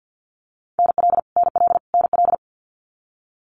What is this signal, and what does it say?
Morse code, heard as a keyed tone, about 50 wpm, 700 Hz. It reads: NL NL NL
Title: Canadian Provinces and Territories - Repeated 3x - 50wpm